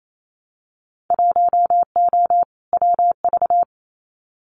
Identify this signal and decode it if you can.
Morse code, heard as a keyed tone, about 28 wpm, 690 Hz. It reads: 1O WV
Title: Winter Field Day Exchanges Encore 28wpm